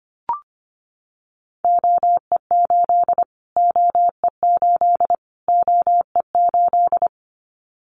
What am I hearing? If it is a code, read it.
Morse code, heard as a keyed tone, about 25 wpm, 700 Hz. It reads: OE8 OE8 OE8
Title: Callsign Prefixes for Top 25 DXCC - Repeated 3x - 25wpm